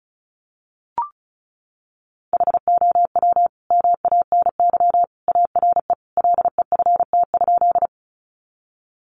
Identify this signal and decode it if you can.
Morse code, heard as a keyed tone, about 34 wpm, 700 Hz. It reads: HOW MANY ARE LEFT?
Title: Sentences from Top 300 Words 35wpm